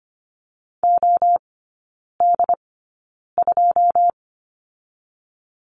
Morse code, heard as a keyed tone, about 25 wpm, spaced as for 9 wpm, 700 Hz. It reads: OD2
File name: Lesson 15e - Number 5 - Three Characters 25wpm